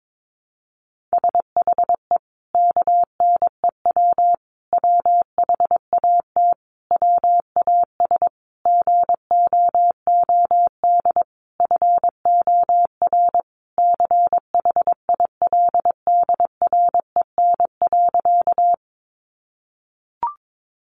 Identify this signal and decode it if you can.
Morse code, heard as a keyed tone, about 22 wpm, 700 Hz. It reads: SHE KNEW WHAT WAS GOOD FOR CHILDREN.
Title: Sentences from Top 400 Words 22wpm